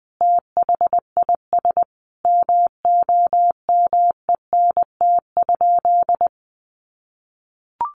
Morse code, heard as a keyed tone, about 20 wpm, 700 Hz. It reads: THIS MOMENT?
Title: Sentences from Top 1000 Words 20wpm